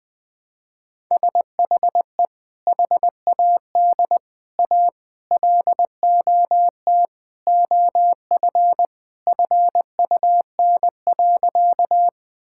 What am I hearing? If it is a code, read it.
Morse code, heard as a keyed tone, about 20 wpm, 700 Hz. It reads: SHE HAD A LOT OF FUN.
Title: Sentences with 4-Word Phrases - 20wpm